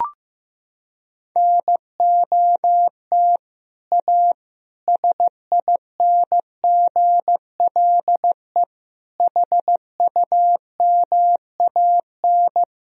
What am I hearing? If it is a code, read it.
Morse code, heard as a keyed tone, about 15 wpm, 700 Hz. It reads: NOT A SINGLE HUMAN
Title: Sentences from Top 1000 Words - Review Part 4 - 15wpm